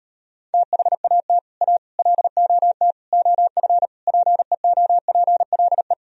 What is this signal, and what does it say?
Morse code, heard as a keyed tone, about 38 wpm, 700 Hz. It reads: THAT A LOT OF PEOPLE